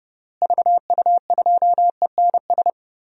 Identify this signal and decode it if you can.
Morse code, heard as a keyed tone, about 30 wpm, 700 Hz. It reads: VU2ENS
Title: DXCC-100 Call Signs - 6 Characters - Repeated 3x - 30wpm